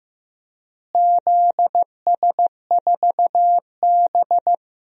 Morse code, heard as a keyed tone, about 15 wpm, 700 Hz. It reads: ZS4B
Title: DXCC-100 Call Signs - 4 Characters - Repeated 2x - 15wpm